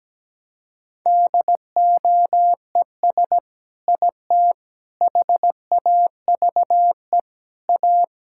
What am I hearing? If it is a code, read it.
Morse code, heard as a keyed tone, about 17 wpm, 700 Hz. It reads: DOES IT HAVE A
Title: Sentences from Top 1000 Words - Review Part 5 - 17wpm